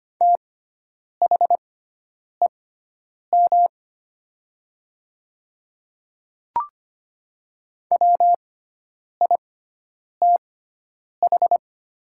Morse code, heard as a keyed tone, about 25 wpm, spaced as for 9 wpm, 700 Hz. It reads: THEM WITH
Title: 6x Character Spacing - Top 100 Words - 25wpm